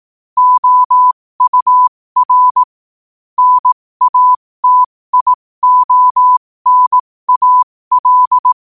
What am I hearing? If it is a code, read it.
Morse code, heard as a keyed tone, about 18 wpm, spaced as for 15 wpm, 1000 Hz. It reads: OUR NATIONAL
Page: New quotes every day in morse code at 15 Words per minute.